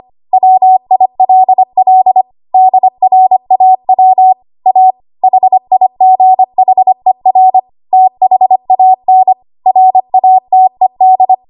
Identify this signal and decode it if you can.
Morse code, heard as a keyed tone, about 25 wpm, 750 Hz. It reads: WILL DRAW A HIGHER THAN RATED
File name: Archive / 25 WPM / 25WPM